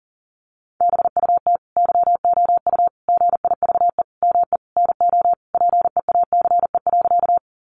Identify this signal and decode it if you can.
Morse code, heard as a keyed tone, about 40 wpm, 700 Hz. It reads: BUT YOU GIVE ME NO PEACE.
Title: Sentences from Top 1500 Words 40wpm